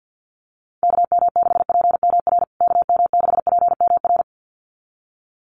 Morse code, heard as a keyed tone, about 50 wpm, 700 Hz. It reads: KM6PMR KM6PMR
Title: USA Call Signs Repeated 2x 50wpm